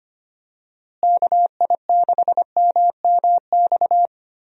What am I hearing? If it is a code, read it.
Morse code, heard as a keyed tone, about 25 wpm, 700 Hz. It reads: KI6MMX
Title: USA Call Signs - 6 Characters - Repeated 3x - 25wpm